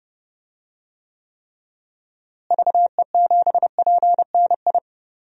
Morse code, heard as a keyed tone, about 30 wpm, 700 Hz. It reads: VE7PNI
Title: DXCC-100 Call Signs - 6 Characters - Repeated 3x - 30wpm